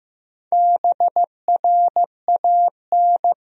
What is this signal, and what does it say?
Morse code, heard as a keyed tone, about 15 wpm, 700 Hz. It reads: BRAN
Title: USA Common Names - Max 5 Letters - 15wpm